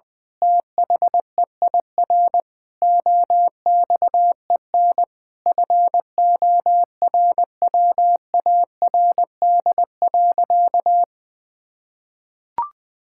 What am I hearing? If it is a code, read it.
Morse code, heard as a keyed tone, about 21 wpm, 700 Hz. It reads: THEIR OXEN FORWARD.
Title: Pangram Challenge v2 - 20wpm